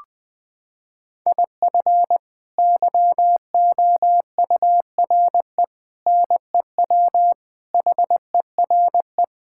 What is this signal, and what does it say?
Morse code, heard as a keyed tone, about 20 wpm, 700 Hz. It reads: IF YOURE NEW HERE
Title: Sets of 8 Words - 20wpm